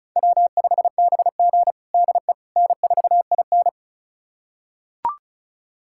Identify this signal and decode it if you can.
Morse code, heard as a keyed tone, about 36 wpm, 700 Hz. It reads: W5BG DE N4IN
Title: USA Callsign DE Callsign - 4 Characters - 35wpm